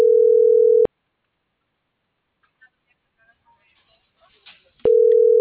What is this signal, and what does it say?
Ringback tone (North America), 440 + 480 Hz.